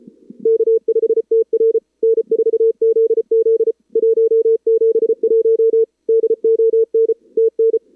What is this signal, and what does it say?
Morse code, heard as a keyed tone, about 34 wpm, 440 Hz. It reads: K5TR N4ZZ 171 DON TN
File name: CWSprint / Feb2022 / k5tr-feb-na-sprint-cw-k5ot -0100